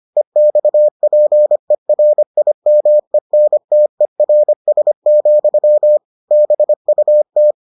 Morse code, heard as a keyed tone, about 25 wpm, 600 Hz. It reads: EXPERIMENTERS, BUT